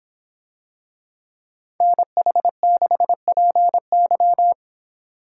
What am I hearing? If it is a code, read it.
Morse code, heard as a keyed tone, about 25 wpm, 700 Hz. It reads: NH6PY